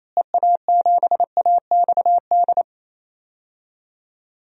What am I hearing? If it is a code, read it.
Morse code, heard as a keyed tone, about 28 wpm, 700 Hz. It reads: EA7AXD
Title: DXCC-100 Call Signs - 6 Characters - Repeated 3x - 28wpm